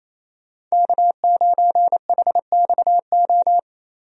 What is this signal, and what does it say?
Morse code, heard as a keyed tone, about 28 wpm, 700 Hz. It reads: K9HXO
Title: USA Call Signs - 5 Characters - 28wpm